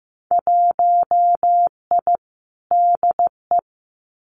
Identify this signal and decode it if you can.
Morse code, heard as a keyed tone, about 15 wpm, 700 Hz. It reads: 1I DE
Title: Winter Field Day Exchanges Encore 15wpm